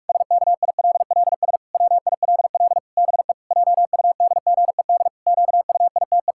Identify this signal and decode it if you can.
Morse code, heard as a keyed tone, about 41 wpm, 700 Hz. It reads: SKILLS WILL BE JUDGED QUITE
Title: Pangram Challenge v2 - 45wpm